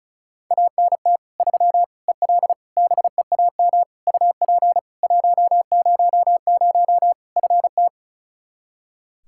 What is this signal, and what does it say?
Morse code, heard as a keyed tone, about 36 wpm, 700 Hz. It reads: ANT 3 EL BEAM UP 100 FT
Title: K1USN SST QSO Practice 35wpm